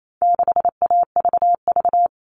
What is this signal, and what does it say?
Morse code, heard as a keyed tone, about 28 wpm, 700 Hz. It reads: 6AVV
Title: All Cut Numbers - 4-Digit 28wpm